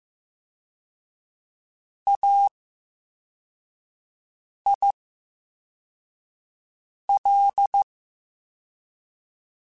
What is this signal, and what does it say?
Morse code, heard as a keyed tone, about 15 wpm, spaced as for 4 wpm, 800 Hz. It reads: AIL
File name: Special INGs-mp3s / Special INGs-mp3s / VerySlowToFaster Ing @13